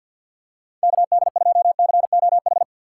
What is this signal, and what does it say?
Morse code, heard as a keyed tone, about 51 wpm, 700 Hz. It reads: KD2COH
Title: USA Call Signs - 3x Character Spacing - 50wpm